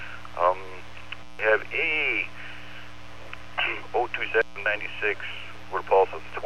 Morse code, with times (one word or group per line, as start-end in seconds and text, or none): none